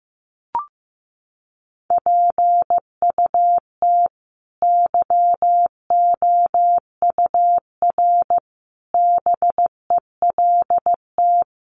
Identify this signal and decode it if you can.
Morse code, heard as a keyed tone, about 15 wpm, 700 Hz. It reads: PUT YOUR BELT
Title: Sentences from Top 1400 Words 15wpm